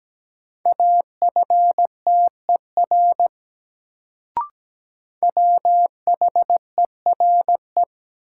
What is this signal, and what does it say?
Morse code, heard as a keyed tone, about 17 wpm, 700 Hz. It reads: AFTER WHERE
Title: Sets of 2 Words - New Word Drill - 17wpm